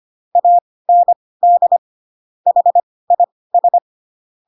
Morse code, heard as a keyed tone, about 25 wpm, spaced as for 18 wpm, 700 Hz. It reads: AND HIS